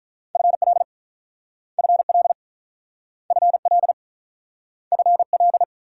Morse code, heard as a keyed tone, about 43 wpm, 700 Hz. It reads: FL FL FL FL
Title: Speed-Racing Multi-letter Phonemes - 35wpm